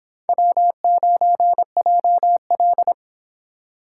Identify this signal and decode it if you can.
Morse code, heard as a keyed tone, about 29 wpm, 700 Hz. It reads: W9JL